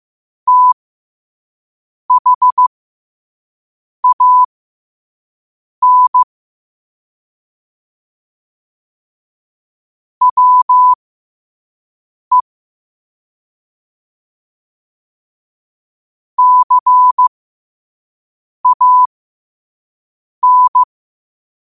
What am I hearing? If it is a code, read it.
Morse code, heard as a keyed tone, about 15 wpm, spaced as for 5 wpm, 1000 Hz. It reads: THAN WE CAN
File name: 5wpm